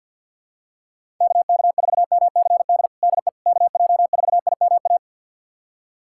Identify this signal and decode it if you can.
Morse code, heard as a keyed tone, about 46 wpm, 700 Hz. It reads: KK4MCD DE KJ4IMA
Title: USA Callsign DE Callsign - 6 Characters - 50wpm